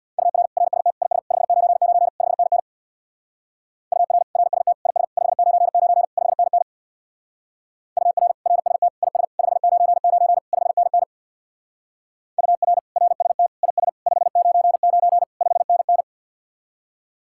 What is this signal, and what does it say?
Morse code, heard as a keyed tone, about 59 wpm, 700 Hz. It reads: UR RST IS 599 5NN UR RST IS 599 5NN UR RST IS 599 5NN UR RST IS 599 5NN
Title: Speed-Racing Common QSO Phrases Encore - 50wpm